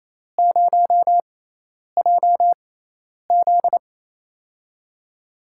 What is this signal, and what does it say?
Morse code, heard as a keyed tone, about 28 wpm, spaced as for 10 wpm, 700 Hz. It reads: 0JZ
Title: USA Call Signs - 4 Characters - 6x Character Spacing - 28wpm